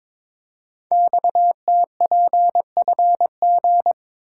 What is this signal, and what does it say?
Morse code, heard as a keyed tone, about 22 wpm, 700 Hz. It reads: XTPFG
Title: Code Groups - 5 Letters - 22wpm